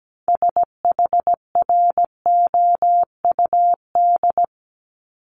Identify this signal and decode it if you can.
Morse code, heard as a keyed tone, about 17 wpm, 700 Hz. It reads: SHROUD
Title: Multi-letter Phonemes 17wpm